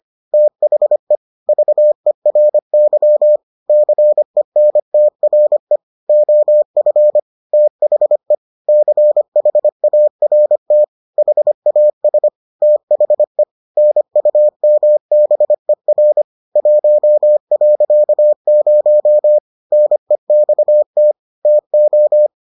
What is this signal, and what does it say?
Morse code, heard as a keyed tone, about 25 wpm, 600 Hz. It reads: THE VERY CENTRE OF THE CHART HAS THE NUMBER 1.0 NEXT TO